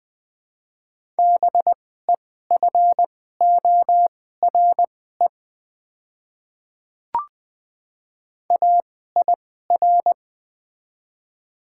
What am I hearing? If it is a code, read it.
Morse code, heard as a keyed tone, about 20 wpm, spaced as for 14 wpm, 700 Hz. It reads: BEFORE AIR
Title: 2x Character Spacing - Top 200 Words - 20wpm